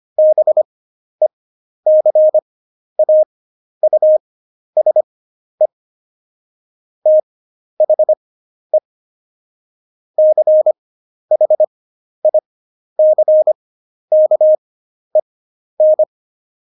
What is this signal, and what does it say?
Morse code, heard as a keyed tone, about 25 wpm, spaced as for 11 wpm, 620 Hz. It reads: BECAUSE THE CHICKEN
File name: Jokes-25-char-10-eff